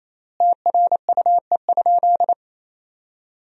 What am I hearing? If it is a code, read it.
Morse code, heard as a keyed tone, about 28 wpm, 700 Hz. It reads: TRUE?
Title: 3x Word-Spacing - Sentences from Top 400 Words - 28wpm